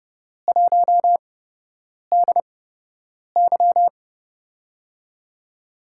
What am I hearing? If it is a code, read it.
Morse code, heard as a keyed tone, about 32 wpm, spaced as for 8 wpm, 700 Hz. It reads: 1DY
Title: USA Call Signs - 4 Characters - 8x Character Spacing - 30wpm